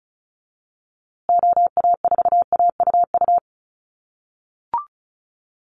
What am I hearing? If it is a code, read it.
Morse code, heard as a keyed tone, about 34 wpm, 700 Hz. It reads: OA4AUU
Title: USA Call Signs Repeated 2x 35wpm